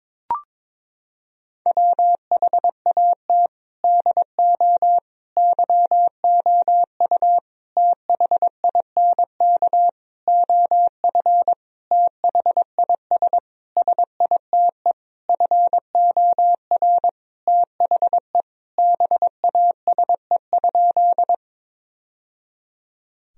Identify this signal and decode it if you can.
Morse code, heard as a keyed tone, about 22 wpm, 700 Hz. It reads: WHAT DO YOU THINK OF THIS SITE FOR THE BASE?
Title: Sentences from Top 1000 Words - Review Part 1 - 22wpm